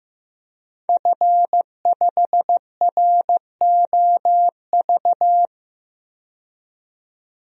Morse code, heard as a keyed tone, about 15 wpm, 700 Hz. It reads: F5ROV
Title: DXCC-100 Call Signs - 5 Characters - 15wpm